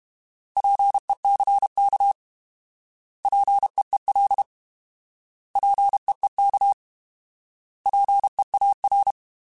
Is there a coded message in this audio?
Yes: Morse code from 0.6 to 9.1 s, heard as a keyed tone, about 32 wpm, 800 Hz. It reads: PECK PEEL PEEK PEAR